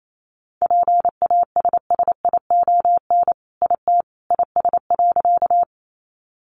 Morse code, heard as a keyed tone, about 28 wpm, 700 Hz. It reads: PASSION IT IS.
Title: Sentences from Top 1300 Words 28wpm